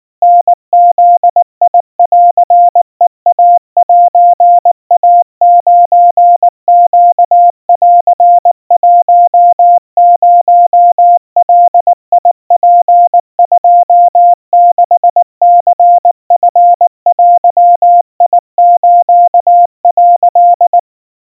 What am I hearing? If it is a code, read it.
Morse code (Japanese Wabun code), heard as a keyed tone, about 19 wpm, 700 Hz. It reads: タブンヘイセイ9ネン10ガツ26ニチデス」
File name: kawa / Morse sample / kamyu 70 cpm 700 Hz Crispy